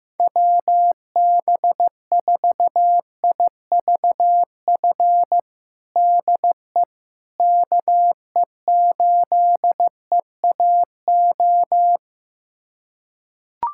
Morse code, heard as a keyed tone, about 15 wpm, 700 Hz. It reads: WB4IVF DE KE8EAO
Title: USA Callsign DE Callsign 15wpm